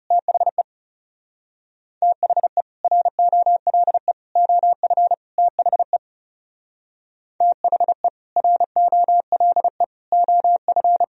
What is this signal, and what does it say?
Morse code, heard as a keyed tone, about 38 wpm, 700 Hz. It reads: THE THE ROLE OF THE THE ROLE OF